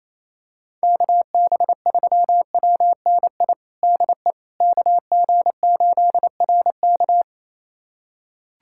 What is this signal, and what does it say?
Morse code, heard as a keyed tone, about 28 wpm, 700 Hz. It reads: KB3WNI DE KG8RK
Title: USA Callsign DE Callsign 28wpm